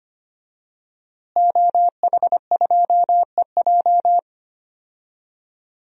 Morse code, heard as a keyed tone, about 25 wpm, 700 Hz. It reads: OH2EJ
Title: DXCC-100 Call Signs - 5 Characters - Repeated 3x - 25wpm